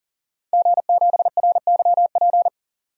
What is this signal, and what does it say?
Morse code, heard as a keyed tone, about 40 wpm, 700 Hz. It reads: G7RYP
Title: DXCC-100 Call Signs - 5 Characters - 40wpm